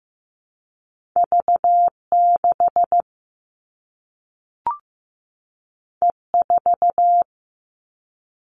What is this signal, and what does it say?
Morse code, heard as a keyed tone, about 15 wpm, 700 Hz. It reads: V6 E4
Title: All Cut Numbers - 2-Digit 15wpm